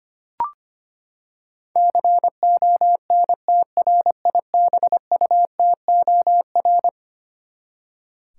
Morse code, heard as a keyed tone, about 25 wpm, 700 Hz. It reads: CONTRIBUTOR
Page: morse-code-ninja-practice-25wpm